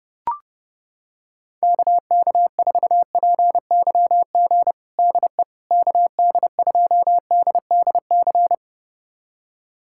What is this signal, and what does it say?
Morse code, heard as a keyed tone, about 30 wpm, 700 Hz. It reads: KK4PYG DE KD2DDC
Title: USA Callsign DE Callsign 30wpm